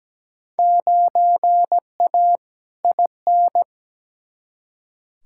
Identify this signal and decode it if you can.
Morse code, heard as a keyed tone, about 17 wpm, 700 Hz. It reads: 9A IN
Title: Field Day Exchanges Encore - 17wpm